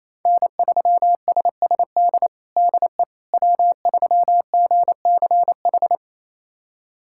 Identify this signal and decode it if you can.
Morse code, heard as a keyed tone, about 28 wpm, 700 Hz. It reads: N3SSD DE W3GCH
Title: USA Callsign DE Callsign - 5 Characters - 28wpm